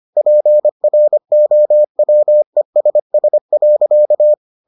Morse code, heard as a keyed tone, about 25 wpm, 600 Hz. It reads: PROWESS.